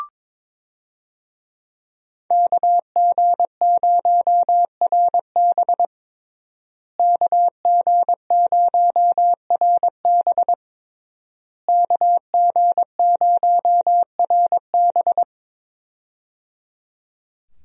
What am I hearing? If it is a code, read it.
Morse code, heard as a keyed tone, about 22 wpm, 700 Hz. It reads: KG0RB KG0RB KG0RB
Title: USA Call Signs - 5 Characters - Repeated 3x - 22wpm